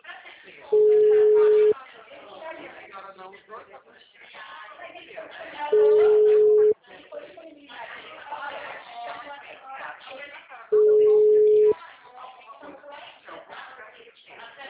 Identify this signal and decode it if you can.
Ringback tone (Europe, 425 Hz), with speech.
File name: out- -2007- - - .51812